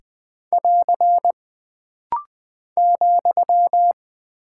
Morse code, heard as a keyed tone, about 20 wpm, 700 Hz. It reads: <AR> ,
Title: Single Punctuation-Prosign - Rapid-Fire - 20wpm